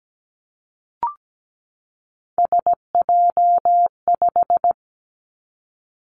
Morse code, heard as a keyed tone, about 17 wpm, 700 Hz. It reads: SJ5
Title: Three Challenging Letter-Number Combinations 17wpm